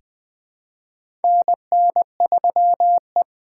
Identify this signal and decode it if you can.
Morse code, heard as a keyed tone, about 20 wpm, 700 Hz. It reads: NN3E
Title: 2023 POTA Activator Call Signs - 20wpm